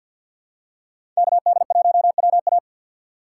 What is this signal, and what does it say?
Morse code, heard as a keyed tone, about 52 wpm, 700 Hz. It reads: KD1WA
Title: USA Call Signs - 5 Characters - 3x Character Spacing - 50wpm